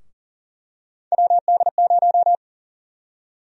Morse code, heard as a keyed tone, about 38 wpm, 700 Hz. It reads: WD0
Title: USA Call Sign Prefixes - Repeated 2x - 40wpm